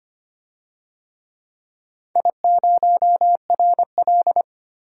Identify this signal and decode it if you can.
Morse code, heard as a keyed tone, about 25 wpm, 700 Hz. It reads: I0RL
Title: DXCC-100 Call Signs - 4 Characters - 25wpm